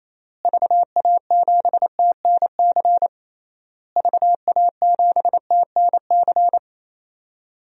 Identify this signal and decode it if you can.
Morse code, heard as a keyed tone, about 28 wpm, 700 Hz. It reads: VA7TNC VA7TNC
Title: DXCC-100 Call Signs - 6 Characters - Repeated 3x - 28wpm